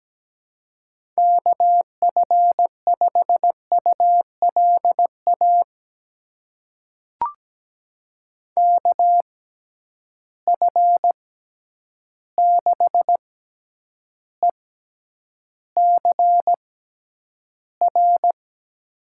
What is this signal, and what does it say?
Morse code, heard as a keyed tone, about 17 wpm, spaced as for 9 wpm, 700 Hz. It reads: KF5ULA KF6ECR
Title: USA Call Signs - 6 Characters - 6x Character Spacing - 17wpm